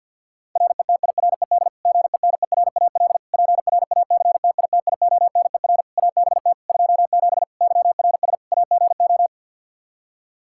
Morse code, heard as a keyed tone, about 52 wpm, 700 Hz. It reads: RETIRED GENERAL PRACTITIONER ABT 27 YRS AGO
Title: Practice QSO - Exchange 3 - More Info - 50wpm